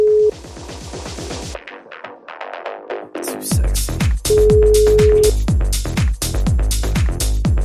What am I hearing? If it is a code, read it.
Ringback tone (Europe, 425 Hz).